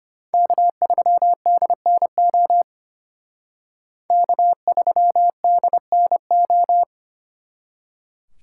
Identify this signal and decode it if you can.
Morse code, heard as a keyed tone, about 33 wpm, 700 Hz. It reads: K3DNO K3DNO